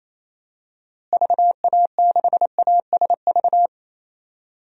Morse code, heard as a keyed tone, about 28 wpm, 700 Hz. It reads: VA6ASV
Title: DXCC-100 Call Signs - 6 Characters - Repeated 2x - 28wpm